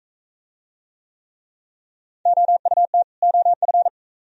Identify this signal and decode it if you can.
Morse code, heard as a keyed tone, about 44 wpm, 690 Hz. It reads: OUT OF